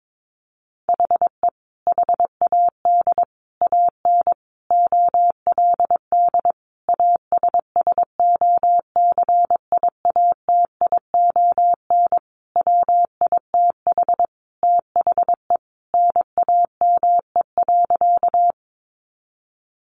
Morse code, heard as a keyed tone, about 22 wpm, 700 Hz. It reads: HE HAD AN OLD ASSOCIATION WITH THE NAME.
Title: Sentences from Top 1100 Words 22wpm